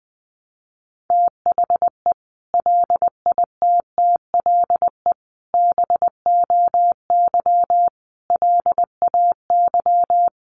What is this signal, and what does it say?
Morse code, heard as a keyed tone, about 20 wpm, 700 Hz. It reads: THE LITTLE BOY LAY
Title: Sentences from Top 500 Words 20wpm